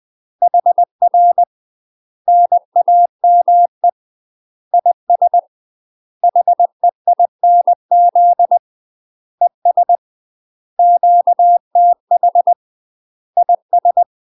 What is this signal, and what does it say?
Morse code, heard as a keyed tone, about 20 wpm, 700 Hz. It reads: HR NAME IS HEINZ ES QTH IS